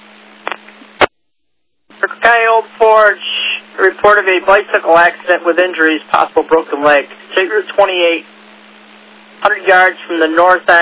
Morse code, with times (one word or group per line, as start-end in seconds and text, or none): none